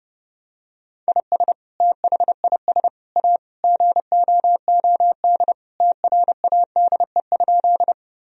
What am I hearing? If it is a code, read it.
Morse code, heard as a keyed tone, about 30 wpm, 700 Hz. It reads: IS THIS A GOOD TRADE?
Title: Sentences from Top 1000 Words - Review Part 4 - 30wpm